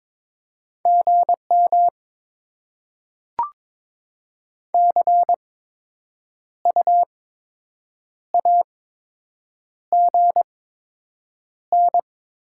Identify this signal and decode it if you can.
Morse code, heard as a keyed tone, about 22 wpm, spaced as for 7 wpm, 700 Hz. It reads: GM CU AGN
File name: Top 20 QSO Elements - 8x Character Spacing - 22wpm